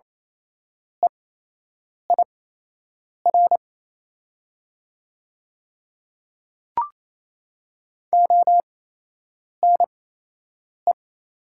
Morse code, heard as a keyed tone, about 27 wpm, spaced as for 8 wpm, 700 Hz. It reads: EIR ONE
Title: 8x Character Spacing - Top 100 Words - 28wpm